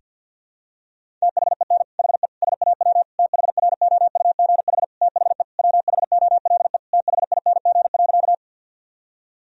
Morse code, heard as a keyed tone, about 51 wpm, 700 Hz. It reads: THEN HE SAW THROUGH THE WHOLE THING.
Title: Sentences from Top 1000 Words - Review Part 1 - 50wpm